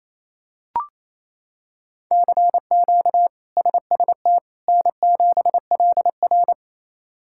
Morse code, heard as a keyed tone, about 28 wpm, 700 Hz. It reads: CQ SST N7LR
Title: K1USN SST QSO Practice - 28wpm